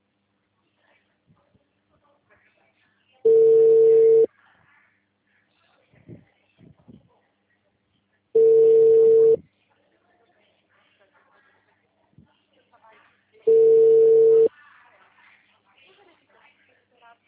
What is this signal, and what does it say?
Ringback tone (China, 450 Hz), with speech.
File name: out- -2024- - - .101